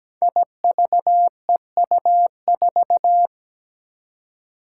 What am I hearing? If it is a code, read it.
Morse code, heard as a keyed tone, about 17 wpm, 700 Hz. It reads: IVEU4
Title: Five Challenging Letter-Number Combinations 17wpm